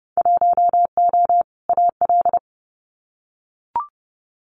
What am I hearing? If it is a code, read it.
Morse code, heard as a keyed tone, about 30 wpm, 700 Hz. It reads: 1O AL
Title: Winter Field Day Exchanges 30wpm